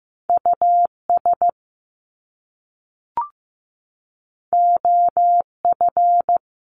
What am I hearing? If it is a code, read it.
Morse code, heard as a keyed tone, about 15 wpm, 700 Hz. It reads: US OF
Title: Top Two Letter Combinations in Words 15wpm